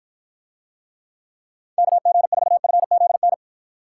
Morse code, heard as a keyed tone, about 53 wpm, 700 Hz. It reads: KG4FZN